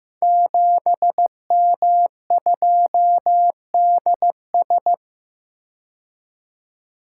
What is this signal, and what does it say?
Morse code, heard as a keyed tone, about 15 wpm, 700 Hz. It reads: 7M2DS
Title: Code Groups - 5 Letter:Numbers - 15wpm